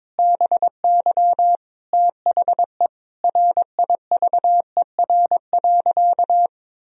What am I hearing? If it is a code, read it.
Morse code, heard as a keyed tone, about 22 wpm, 700 Hz. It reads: BY THE RIVER.
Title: Sentences from Top 1000 Words - Review Part 1 - 22wpm